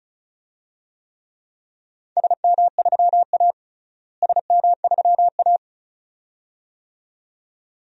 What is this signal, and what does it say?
Morse code, heard as a keyed tone, about 36 wpm, 700 Hz. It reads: SM3A SM3A
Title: DXCC-100 Call Signs - 4 Characters - Repeated 2x - 35wpm